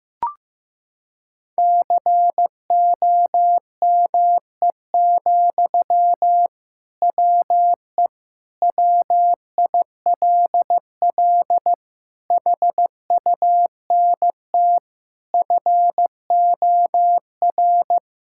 Morse code, heard as a keyed tone, about 15 wpm, 700 Hz. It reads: COME, WE WILL HUNT FOR
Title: Sentences from Top 1600 Words 15wpm